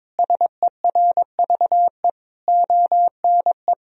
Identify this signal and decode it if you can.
Morse code, heard as a keyed tone, about 21 wpm, 700 Hz. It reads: SERVE ONE
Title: Sentences from Top 1000 Words - Review Part 2 - 22wpm